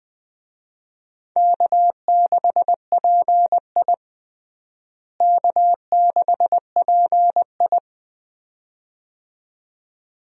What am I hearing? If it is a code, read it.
Morse code, heard as a keyed tone, about 20 wpm, 700 Hz. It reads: K6PI K6PI
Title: USA Call Signs - 4 Characters - Repeated 2x - 20wpm